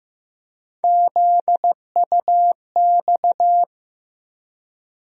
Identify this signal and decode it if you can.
Morse code, heard as a keyed tone, about 15 wpm, 700 Hz. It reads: ZUX
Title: USA Call Sign Postfixes - 3 Letters 15wpm